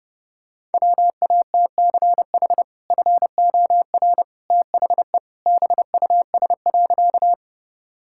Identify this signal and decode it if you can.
Morse code, heard as a keyed tone, about 30 wpm, 700 Hz. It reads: WATCH FOR THE BUS.
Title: Sentences from Top 1000 Words - Review Part 6 - 30wpm